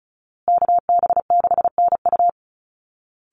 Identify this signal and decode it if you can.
Morse code, heard as a keyed tone, about 34 wpm, 700 Hz. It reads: KB6NU
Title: Call Signs Repeated 2x - 35wpm